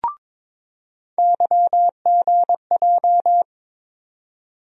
Morse code, heard as a keyed tone, about 22 wpm, 700 Hz. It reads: YGJ
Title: USA Call Sign Postfixes - 3 Letters 22wpm